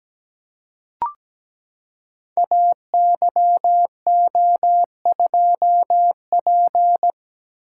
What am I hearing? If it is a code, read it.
Morse code, heard as a keyed tone, about 17 wpm, 700 Hz. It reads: AYO2P
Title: Code Groups - 5 Letter:Numbers - 17wpm